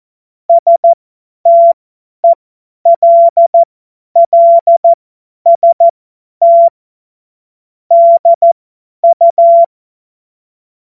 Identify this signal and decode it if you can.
Morse code, heard as a keyed tone, about 14 wpm, spaced as for 10 wpm, 680 Hz. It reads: STELLST DU
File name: Morsekurs / Witze / Witz 26. Oktober 2021, 50 F70